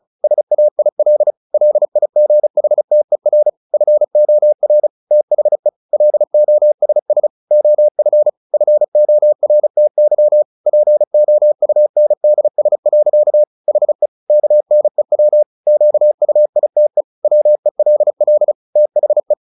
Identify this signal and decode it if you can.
Morse code, heard as a keyed tone, about 34 wpm, 600 Hz. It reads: SAIL LIGHTER FOR THE LOSS OF FORTY POUNDS. HE KNEW QUITE WELL THE
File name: the old man and the sea 2 35wpm